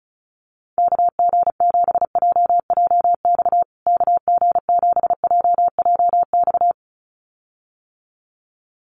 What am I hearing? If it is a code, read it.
Morse code, heard as a keyed tone, about 34 wpm, 700 Hz. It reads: KG7JJX KG7JJX
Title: USA Call Signs Repeated 2x 35wpm